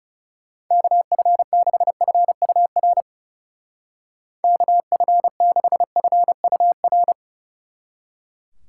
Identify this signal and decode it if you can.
Morse code, heard as a keyed tone, about 39 wpm, 700 Hz. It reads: KF6FUR KF6FUR